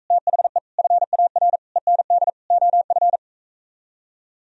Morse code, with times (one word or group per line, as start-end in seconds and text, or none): 0.1-0.6 THE
0.8-1.6 FAR
1.8-2.3 END
2.5-3.2 OF